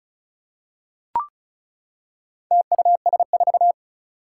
Morse code, heard as a keyed tone, about 36 wpm, 690 Hz. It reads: TUS4
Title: Four Challenging Letter-Number Combinations 35wpm